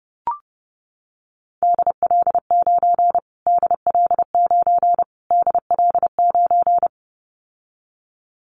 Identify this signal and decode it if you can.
Morse code, heard as a keyed tone, about 28 wpm, 700 Hz. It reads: DL9 DL9 DL9
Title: Callsign Prefixes for Top 25 DXCC - Repeated 3x - 30wpm